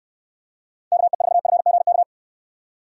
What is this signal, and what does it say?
Morse code, heard as a keyed tone, about 68 wpm, 700 Hz. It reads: KE4FGK